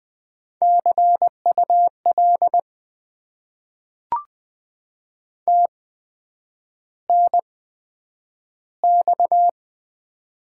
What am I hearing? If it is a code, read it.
Morse code, heard as a keyed tone, about 20 wpm, spaced as for 9 wpm, 700 Hz. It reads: CUL TNX
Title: Top 90 QSO Elements - 8x Character Spacing - 20wpm